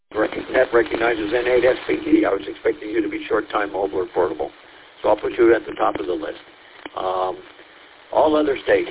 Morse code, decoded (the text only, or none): none